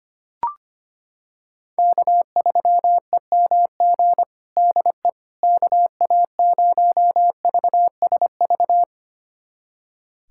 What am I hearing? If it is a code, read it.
Morse code, heard as a keyed tone, about 25 wpm, 700 Hz. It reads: K3EMG DE KA0VSV
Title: USA Callsign DE Callsign 25wpm